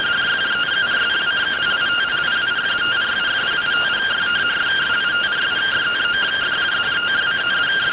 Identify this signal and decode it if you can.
Two-tone teleprinter signal, about 170 Hz shift.